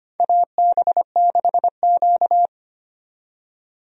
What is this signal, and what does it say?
Morse code, heard as a keyed tone, about 25 wpm, 700 Hz. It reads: AB6Q
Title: USA Call Signs - 4 Characters - 25wpm